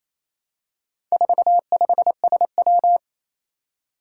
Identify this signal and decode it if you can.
Morse code, heard as a keyed tone, about 28 wpm, 700 Hz. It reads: 45SW
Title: Four Challenging Letter-Number Combinations 28wpm